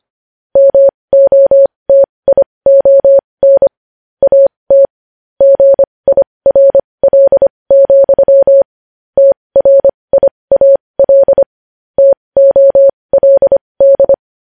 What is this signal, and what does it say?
Morse code, heard as a keyed tone, about 25 wpm, spaced as for 20 wpm, 550 Hz. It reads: MOTION AT GIRL, TRIAL TOLD
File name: podcast / News headlines-1x-25x20